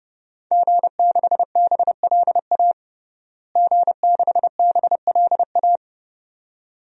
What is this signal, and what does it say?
Morse code, heard as a keyed tone, about 32 wpm, 700 Hz. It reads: G6BLA G6BLA
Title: DXCC-100 Call Signs - 5 Characters - Repeated 2x - 30wpm